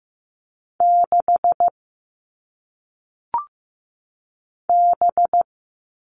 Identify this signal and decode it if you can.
Morse code, heard as a keyed tone, about 15 wpm, 700 Hz. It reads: 6 B
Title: All Cut Numbers - 1-Digit 15wpm